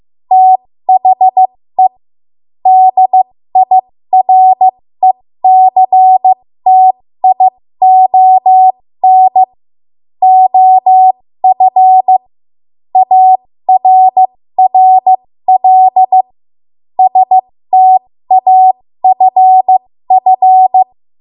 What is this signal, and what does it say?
Morse code, heard as a keyed tone, about 15 wpm, spaced as for 13 wpm, 750 Hz. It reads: THE DIRECTION OF ARRL STAFF
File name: Morse / 13WPM